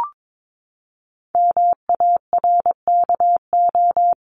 Morse code, heard as a keyed tone, about 22 wpm, 700 Hz. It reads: MARKO
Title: CWT Exchange 22wpm